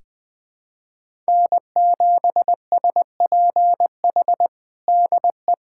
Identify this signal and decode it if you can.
Morse code, heard as a keyed tone, about 20 wpm, 700 Hz. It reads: N7SPH DE
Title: USA Callsign DE Callsign - 5 Characters - 20wpm